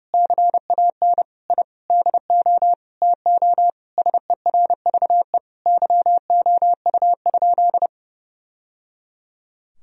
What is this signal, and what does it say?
Morse code, heard as a keyed tone, about 32 wpm, 700 Hz. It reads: CAN I DO TO SERVE YOU?
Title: Sentences from Top 1000 Words - Review Part 2 - 30wpm